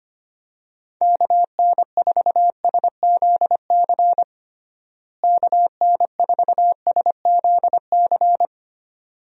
Morse code, heard as a keyed tone, about 25 wpm, 700 Hz. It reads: KN4SZC KN4SZC
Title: USA Call Signs - 6 Characters - Repeated 2x - 25wpm